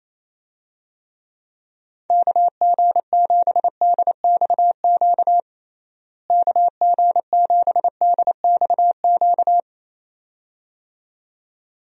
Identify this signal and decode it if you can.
Morse code, heard as a keyed tone, about 28 wpm, 700 Hz. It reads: KG7DXQ KG7DXQ
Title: USA Call Signs - 6 Characters - Repeated 2x - 28wpm